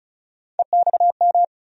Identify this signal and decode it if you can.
Morse code, heard as a keyed tone, about 36 wpm, 700 Hz. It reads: EXM
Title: USA Call Sign Postfixes - 3 Letters 35wpm